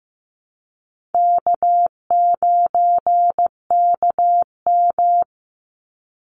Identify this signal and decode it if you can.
Morse code, heard as a keyed tone, about 15 wpm, 700 Hz. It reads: K9KM
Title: Call Signs Repeated 3x - 15wpm